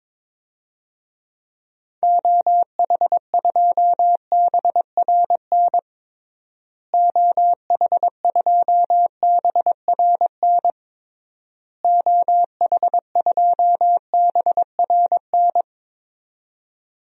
Morse code, heard as a keyed tone, about 22 wpm, 700 Hz. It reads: OH2BRN OH2BRN OH2BRN
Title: DXCC-100 Call Signs - 6 Characters - Repeated 3x - 22wpm